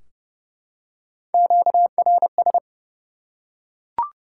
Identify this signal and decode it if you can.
Morse code, heard as a keyed tone, about 30 wpm, 700 Hz. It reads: QRS
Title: Top 90 QSO Elements 30wpm